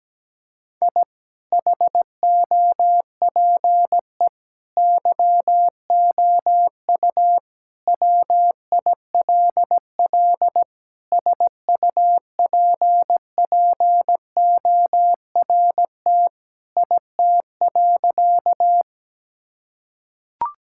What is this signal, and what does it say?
Morse code, heard as a keyed tone, about 17 wpm, 700 Hz. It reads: I HOPE YOU WILL SUPPORT IT.
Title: Sentences from Top 1000 Words - Review Part 5 - 17wpm